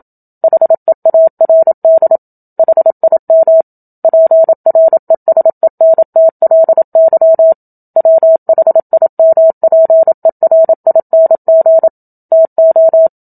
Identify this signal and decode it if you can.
Morse code, heard as a keyed tone, about 27 wpm, 650 Hz. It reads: HEARD HIM PRESENTLY WHIMPERING TO
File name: Book Two / 01-Chapter 1 - 27 WPM